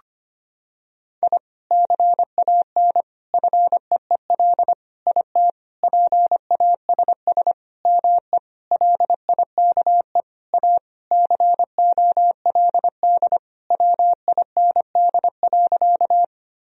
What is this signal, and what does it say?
Morse code, heard as a keyed tone, about 25 wpm, 700 Hz. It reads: I CAN FEEL IT PASS ME LIKE A COLD WIND.
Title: Sentences from Top 1000 Words - Review Part 2 - 25wpm